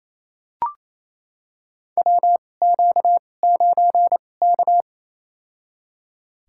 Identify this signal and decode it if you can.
Morse code, heard as a keyed tone, about 28 wpm, spaced as for 20 wpm, 700 Hz. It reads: WQ9K
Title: USA Call Signs - 4 Characters - 2x Character Spacing - 28wpm